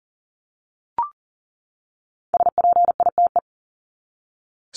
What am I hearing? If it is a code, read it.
Morse code, heard as a keyed tone, about 40 wpm, 700 Hz. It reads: SPITE
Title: Top 1751-2000 Words 40wpm